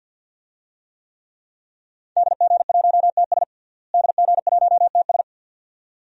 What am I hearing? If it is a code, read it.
Morse code, heard as a keyed tone, about 48 wpm, 700 Hz. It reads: DG1TS DG1TS
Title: DXCC-100 Call Signs - 5 Characters - Repeated 2x - 50wpm